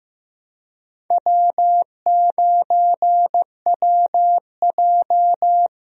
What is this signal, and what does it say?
Morse code, heard as a keyed tone, about 15 wpm, 700 Hz. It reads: W9WJ
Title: USA Call Signs - 4 Characters - Repeated 3x - 15wpm